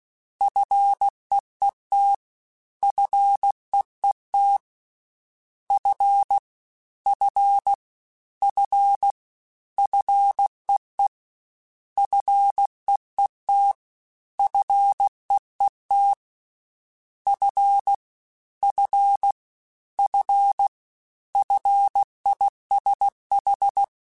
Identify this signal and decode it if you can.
Morse code, heard as a keyed tone, about 16 wpm, 800 Hz. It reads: FEET FEET F F F FEE FEET FEET F F F FISH
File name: Single F Prefixes 3x2 Twice @15